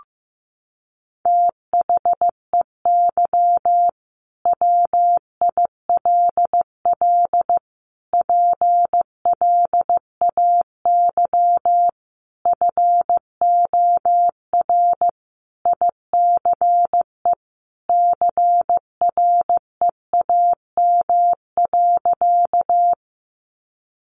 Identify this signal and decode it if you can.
Morse code, heard as a keyed tone, about 15 wpm, 700 Hz. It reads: THEY WILL PLAY FOR ICE CREAM.
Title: Sentences from Top 1500 Words 15wpm